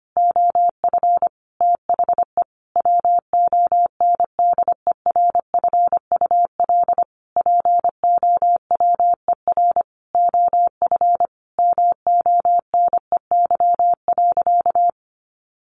Morse code, heard as a keyed tone, about 25 wpm, 690 Hz. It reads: OF THE WONDERFUL POWER OF MONEY.
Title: Sentences from Top 1100 Words 25wpm